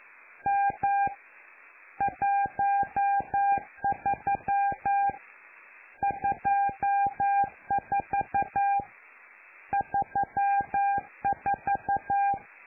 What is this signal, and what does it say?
Morse code, heard as a keyed tone, about 11 wpm, 800 Hz. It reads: M 13 24 34
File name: lu4aao / Globo Amsat